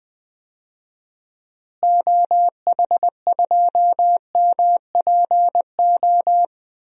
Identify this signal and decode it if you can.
Morse code, heard as a keyed tone, about 20 wpm, 700 Hz. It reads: OH2MPO
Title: DXCC-100 Call Signs - 6 Characters - Repeated 3x - 20wpm